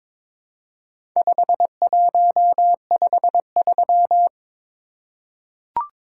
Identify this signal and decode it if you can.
Morse code, heard as a keyed tone, about 22 wpm, 700 Hz. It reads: 5153
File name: practice / 22wpm / Four Letter-Number Combinations - 22wpm